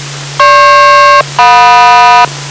Aircraft SELCAL, heard as two pairs of tones. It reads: GP-JM